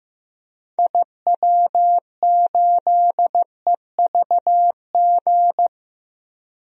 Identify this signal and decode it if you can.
Morse code, heard as a keyed tone, about 15 wpm, 700 Hz. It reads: IW8EVG
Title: DXCC-100 Call Signs - 6 Characters - Repeated 2x - 15wpm